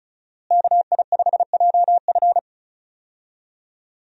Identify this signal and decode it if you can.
Morse code, heard as a keyed tone, about 34 wpm, 700 Hz. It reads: KI5JF
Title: USA Call Signs - 5 Characters - Repeated 3x - 35wpm